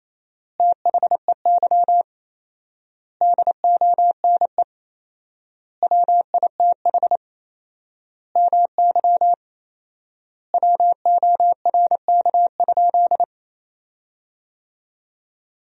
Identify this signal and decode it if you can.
Morse code, heard as a keyed tone, about 27 wpm, 700 Hz. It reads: THEY DONE WITH MY WORK?
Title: 4x Word-Spacing - Sentences from Top 500 Words - 28wpm